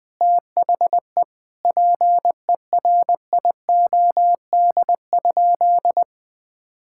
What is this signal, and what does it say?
Morse code, heard as a keyed tone, about 20 wpm, 700 Hz. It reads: THE PERIOD?
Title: Sentences from Top 1000 Words - Review Part 5 - 20wpm